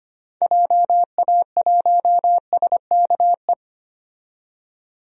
Morse code, heard as a keyed tone, about 25 wpm, 700 Hz. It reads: JA1SKE
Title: DXCC-100 Call Signs - 6 Characters - 25wpm